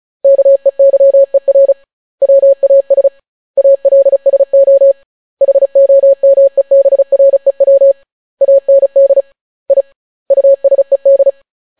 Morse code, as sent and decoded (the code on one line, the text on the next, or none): -.- . -.-- . .-. / .-- .- ... / .- .-.. ... --- / .... --- -- . -... .-. .-- / .- -. -.. / .. / ..- ... -..
KEYER WAS ALSO HOMEBREW AND I USED